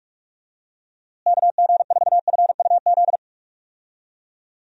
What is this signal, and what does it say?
Morse code, heard as a keyed tone, about 45 wpm, 700 Hz. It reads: KG4FUZ
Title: USA Call Signs - 6 Characters - Repeated 2x - 45wpm